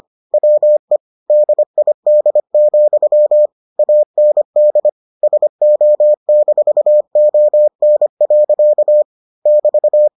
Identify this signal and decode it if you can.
Morse code, heard as a keyed tone, about 25 wpm, 600 Hz. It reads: WE DID, AND SO-ON. <BT>